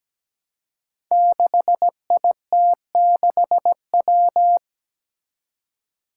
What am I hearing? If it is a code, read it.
Morse code, heard as a keyed tone, about 17 wpm, 700 Hz. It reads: 6IT6W
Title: Five Challenging Letter-Number Combinations 17wpm